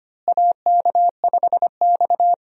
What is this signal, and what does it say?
Morse code, heard as a keyed tone, about 25 wpm, 700 Hz. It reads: AK5X
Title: USA Call Signs - 4 Characters - Repeated 2x - 25wpm